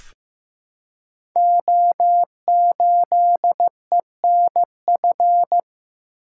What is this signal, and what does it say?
Morse code, heard as a keyed tone, about 15 wpm, 700 Hz. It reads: O8ENF
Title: Code Groups - 5 Letter:Numbers - 15wpm